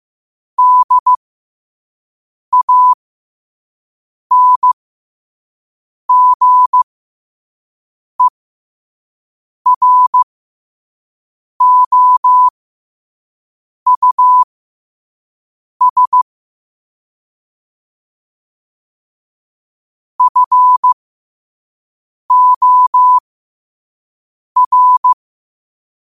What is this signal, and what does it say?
Morse code, heard as a keyed tone, about 15 wpm, spaced as for 5 wpm, 1000 Hz. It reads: DANGEROUS FOR
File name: 5wpm